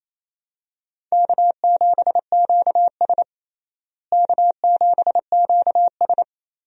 Morse code, heard as a keyed tone, about 28 wpm, 700 Hz. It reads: K7QS K7QS
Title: USA Call Signs - 4 Characters - Repeated 3x - 28wpm